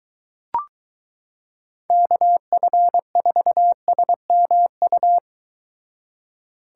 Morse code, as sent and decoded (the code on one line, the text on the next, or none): -.- ..-. ....- ... -- ..-
KF4SMU